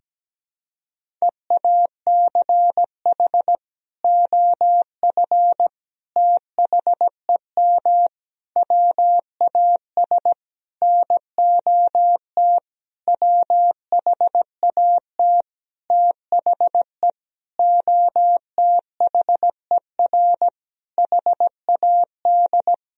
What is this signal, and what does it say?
Morse code, heard as a keyed tone, about 17 wpm, 700 Hz. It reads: EACH OF THEM WAS NOT WHAT THE OTHER HAD
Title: Sentences from Top 1000 Words - Review Part 1 - 17wpm